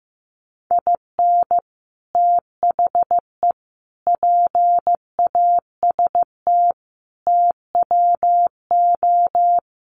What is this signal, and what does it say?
Morse code, heard as a keyed tone, about 15 wpm, 700 Hz. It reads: IN THE PAST TWO
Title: Sets of 5 Words 15wpm